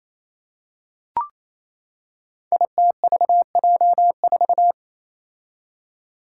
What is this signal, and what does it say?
Morse code, heard as a keyed tone, about 27 wpm, 700 Hz. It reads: ITVJ4
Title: Five Challenging Letter-Number Combinations 28wpm